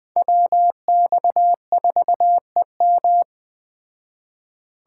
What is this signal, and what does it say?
Morse code, heard as a keyed tone, about 20 wpm, 700 Hz. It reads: WX4EM
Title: Speed Racing - USA 2x2 Call Signs 17wpm